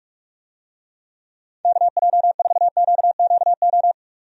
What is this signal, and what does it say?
Morse code, heard as a keyed tone, about 46 wpm, 700 Hz. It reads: KJ4QQO
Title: USA Call Signs - 6 Characters - 45wpm